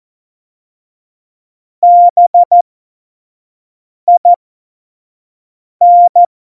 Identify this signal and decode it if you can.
Morse code, heard as a keyed tone, about 14 wpm, spaced as for 5 wpm, 700 Hz. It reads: BIN
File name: Morsekurs / Witze / Witz 12. Dezember 2017, 25 F70